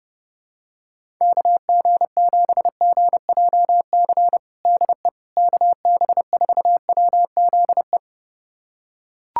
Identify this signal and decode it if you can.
Morse code, heard as a keyed tone, about 30 wpm, 700 Hz. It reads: KG7GJC DE KB4WZE
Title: USA Callsign DE Callsign 30wpm